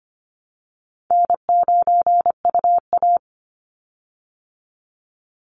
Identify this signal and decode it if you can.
Morse code, heard as a keyed tone, about 25 wpm, 700 Hz. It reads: N9UA
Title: Easy Call Signs Repeated 3x - 25wpm